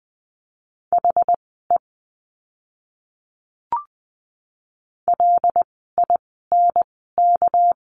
Morse code, heard as a keyed tone, about 20 wpm, spaced as for 14 wpm, 700 Hz. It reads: HE LINK
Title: 199 Lesson 32f - Letter k - Words 20wpm